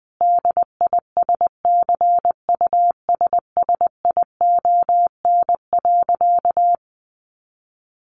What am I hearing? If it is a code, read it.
Morse code, heard as a keyed tone, about 20 wpm, 700 Hz. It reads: DISCUSSION.
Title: Sentences from Top 1500 Words 20wpm